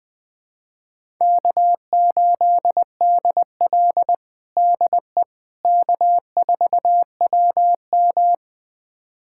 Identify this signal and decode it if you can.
Morse code, heard as a keyed tone, about 21 wpm, 700 Hz. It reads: K8DL DE K4WM
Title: USA Callsign DE Callsign - 4 Characters - 20wpm